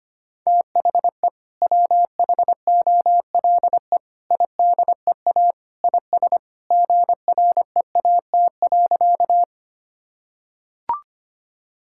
Morse code, heard as a keyed tone, about 25 wpm, 700 Hz. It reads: THE WHOLE IDEA IS GREAT.
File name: practice / 25wpm / Sentences with 2-Word Phrases - 25wpm